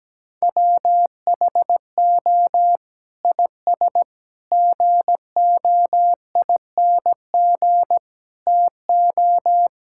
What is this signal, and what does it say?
Morse code, heard as a keyed tone, about 17 wpm, 700 Hz. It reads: WHO IS GOING TO
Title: Sentences from Top 1000 Words - Review Part 5 - 17wpm